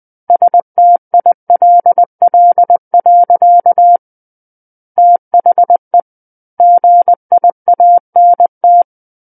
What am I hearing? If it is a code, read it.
Morse code, heard as a keyed tone, about 20 wpm, 700 Hz. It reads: STILL. THE GIANT
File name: 20 WPM